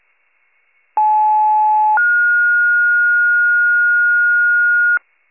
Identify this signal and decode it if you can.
Two-tone page, heard as a one-second tone, then a three-second tone, about 860 then 1430 Hz, with noise. It is heard from 1.0 to 5.0 s.
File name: pager / 75 FIRE / wcalert